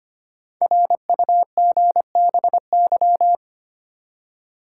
Morse code, heard as a keyed tone, about 25 wpm, 700 Hz. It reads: RUGBY
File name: 5 Letter Words - First 3 Letters Given - 25wpm